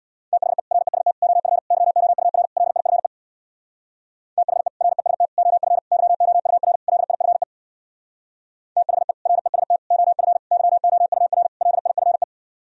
Morse code, heard as a keyed tone, about 64 wpm, 700 Hz. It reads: THE REST OF YOUR LIFE THE REST OF YOUR LIFE THE REST OF YOUR LIFE